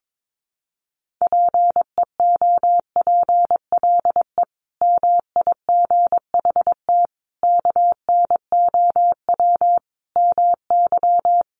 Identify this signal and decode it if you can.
Morse code, heard as a keyed tone, about 22 wpm, 700 Hz. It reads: PEOPLE MIGHT KNOW MY
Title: Sentences from Top 500 Words 22wpm